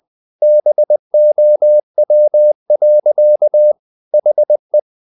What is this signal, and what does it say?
Morse code, heard as a keyed tone, about 20 wpm, 600 Hz. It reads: BOW. HE
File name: the old man and the sea / the old man and the sea part 1 20wpm